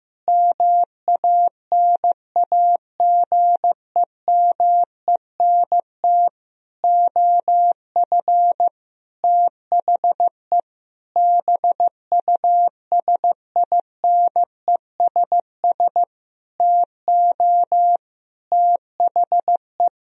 Morse code, heard as a keyed tone, about 15 wpm, 700 Hz. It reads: MANAGEMENT OF THE BUSINESS TO THE
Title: Sentences from Top 1000 Words - Review Part 4 - 15wpm